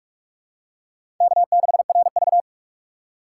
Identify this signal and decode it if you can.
Morse code, heard as a keyed tone, about 46 wpm, 700 Hz. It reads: K6RV
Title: USA Call Signs - 4 Characters - 4x Character Spacing - 45wpm